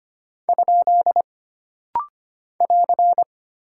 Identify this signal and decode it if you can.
Morse code, heard as a keyed tone, about 25 wpm, 700 Hz. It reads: ? <AR>
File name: Single Punctuation-Prosign - Rapid-Fire - 25wpm